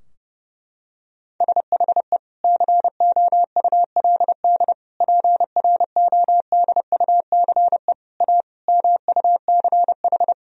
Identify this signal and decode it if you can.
Morse code, heard as a keyed tone, about 30 wpm, 700 Hz. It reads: SHE COULD PRODUCE A MUCH
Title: Sentences from Top 1000 Words - Review Part 3 - 30wpm